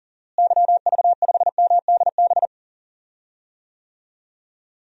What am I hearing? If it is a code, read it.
Morse code, heard as a keyed tone, about 40 wpm, 700 Hz. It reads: YV5MDB
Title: DXCC-100 Call Signs - 6 Characters - Repeated 2x - 40wpm